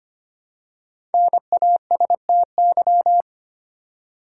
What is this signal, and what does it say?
Morse code, heard as a keyed tone, about 25 wpm, 700 Hz. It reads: NASTY
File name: 5 Letter Words - First 3 Letters Given - 25wpm